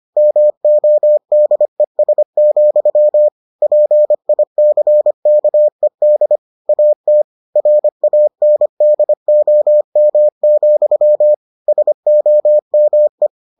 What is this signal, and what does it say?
Morse code, heard as a keyed tone, about 25 wpm, 600 Hz. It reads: MODES, PICKED AT RANDOM, SOME